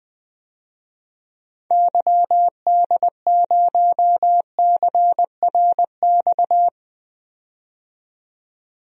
Morse code, heard as a keyed tone, about 20 wpm, 700 Hz. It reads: YD0CRX
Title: DXCC-100 Call Signs - 6 Characters - 20wpm